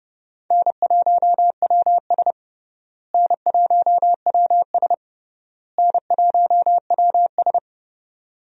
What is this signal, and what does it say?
Morse code, heard as a keyed tone, about 30 wpm, 700 Hz. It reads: N1WS N1WS N1WS
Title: USA Call Signs - 4 Characters - Repeated 3x - 30wpm